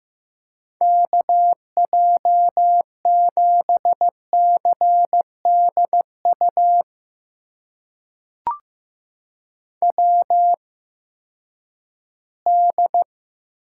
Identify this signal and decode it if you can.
Morse code, heard as a keyed tone, about 15 wpm, spaced as for 10 wpm, 700 Hz. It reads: KJ7CDU WD
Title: USA Call Signs - 6 Characters - 8x Character Spacing - 15wpm